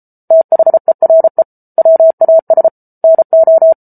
Morse code, heard as a keyed tone, about 34 wpm, 650 Hz. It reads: THERE WAS NO
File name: Book Two / 07-Chapter 7 - 33 WPM